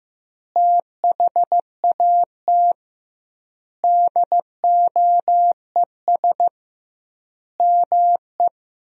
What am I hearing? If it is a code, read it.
Morse code, heard as a keyed tone, about 15 wpm, 700 Hz. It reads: THAT DOES ME
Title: 2x Word-Spacing - Sentences from Top 200 Words - 15wpm